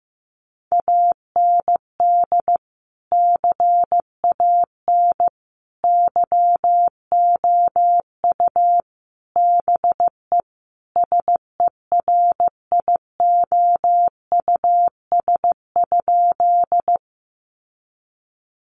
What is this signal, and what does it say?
Morse code, heard as a keyed tone, about 14 wpm, 700 Hz. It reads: AND CAN YOU BE SERIOUS?
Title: Sentences from Top 1100 Words 15wpm